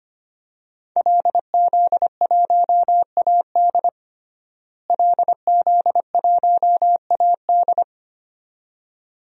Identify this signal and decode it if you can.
Morse code, heard as a keyed tone, about 25 wpm, 700 Hz. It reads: LZ1AD LZ1AD
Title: DXCC-100 Call Signs - 5 Characters - Repeated 2x - 25wpm